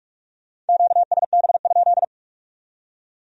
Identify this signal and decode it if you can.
Morse code, heard as a keyed tone, about 57 wpm, 700 Hz. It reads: QSB?